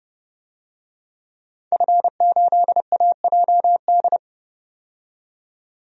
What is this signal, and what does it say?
Morse code, heard as a keyed tone, about 32 wpm, 700 Hz. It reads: F8AJD
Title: DXCC-100 Call Signs - 5 Characters - 30wpm